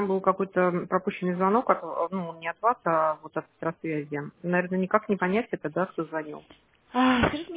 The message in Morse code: none

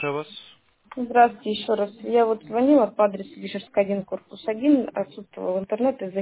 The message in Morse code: none

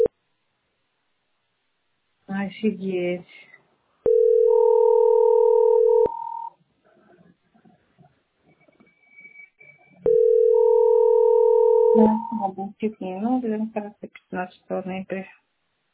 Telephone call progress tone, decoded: ringback tone

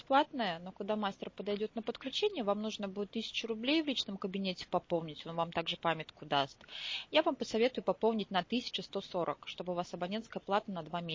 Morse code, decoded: none

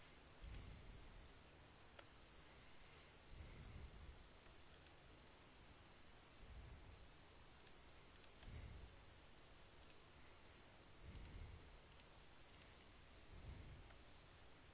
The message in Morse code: none

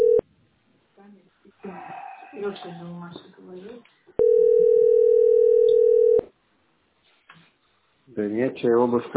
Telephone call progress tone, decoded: ringback tone

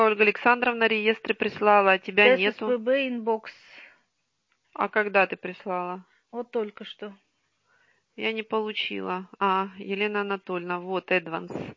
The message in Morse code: none

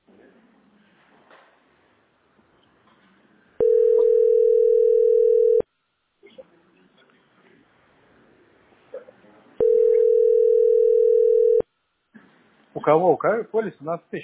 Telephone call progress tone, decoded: ringback tone

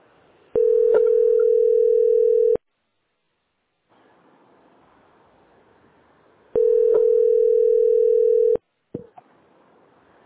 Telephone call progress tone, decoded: ringback tone

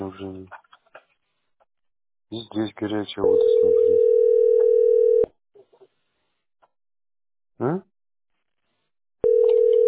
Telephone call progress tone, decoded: ringback tone

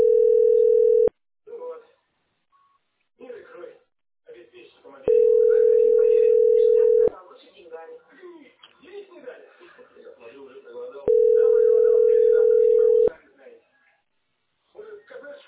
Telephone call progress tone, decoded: ringback tone

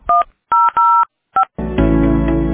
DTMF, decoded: 1005

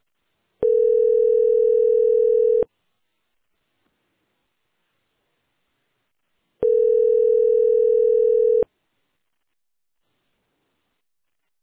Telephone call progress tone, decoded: ringback tone

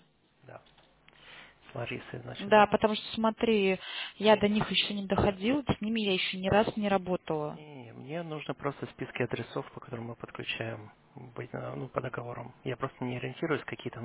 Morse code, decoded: none

